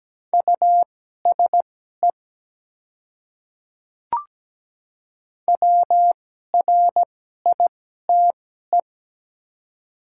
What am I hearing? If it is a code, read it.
Morse code, heard as a keyed tone, about 17 wpm, spaced as for 12 wpm, 700 Hz. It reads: USE WRITE
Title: 2x Character Spacing - Top 100 Words - 17wpm